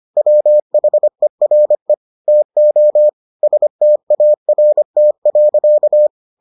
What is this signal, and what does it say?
Morse code, heard as a keyed tone, about 25 wpm, 600 Hz. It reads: WHERE TO START.